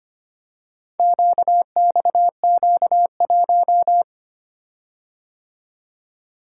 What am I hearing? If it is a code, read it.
Morse code, heard as a keyed tone, about 25 wpm, 700 Hz. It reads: QXQ1
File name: practice / 25wpm / Four Letter-Number Combinations - 25wpm